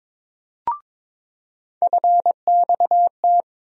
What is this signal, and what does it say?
Morse code, heard as a keyed tone, about 22 wpm, 700 Hz. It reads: FXT
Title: USA Call Sign Postfixes - 3 Letters 22wpm